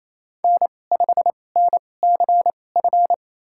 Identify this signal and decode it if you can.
Morse code, heard as a keyed tone, about 28 wpm, spaced as for 20 wpm, 700 Hz. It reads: N5NCF
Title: USA Call Signs - 5 Characters - 2x Character Spacing - 28wpm